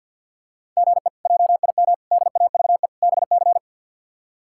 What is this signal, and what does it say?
Morse code, heard as a keyed tone, about 51 wpm, 690 Hz. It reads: GE JIM DAVE BC
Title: K1USN SST QSO Practice - 50wpm